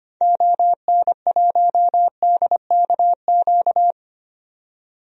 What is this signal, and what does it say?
Morse code, heard as a keyed tone, about 25 wpm, 700 Hz. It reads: ON1DKQ
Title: DXCC-100 Call Signs - 6 Characters - 25wpm